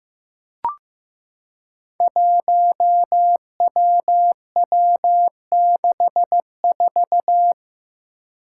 Morse code, heard as a keyed tone, about 15 wpm, 700 Hz. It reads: 1WW64
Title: Five Challenging Letter-Number Combinations 15wpm